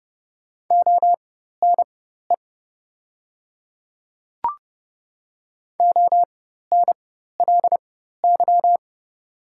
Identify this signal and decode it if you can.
Morse code, heard as a keyed tone, about 30 wpm, spaced as for 14 wpm, 700 Hz. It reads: ONE ONLY
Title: 4x Character Spacing - Top 200 Words - 30wpm